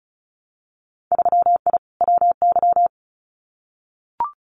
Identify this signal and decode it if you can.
Morse code, heard as a keyed tone, about 36 wpm, 700 Hz. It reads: 3I WY
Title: Winter Field Day Exchanges Encore 35wpm